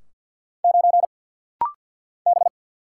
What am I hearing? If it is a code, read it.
Morse code, heard as a keyed tone, about 49 wpm, 690 Hz. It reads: 9 B
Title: Single Letter-Number - Rapid-Fire 50wpm